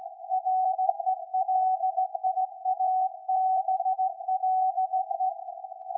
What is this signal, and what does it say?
Morse code, heard as a keyed tone, about 17 wpm, 750 Hz. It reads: RELIABLE
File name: iz7slz / VLF / SAQ 13-02-2023